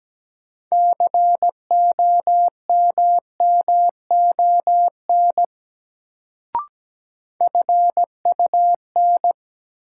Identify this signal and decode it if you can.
Morse code, heard as a keyed tone, about 17 wpm, 700 Hz. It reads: COMMON FUN
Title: Sentences from Top 700 Words - New Word Drill - 17wpm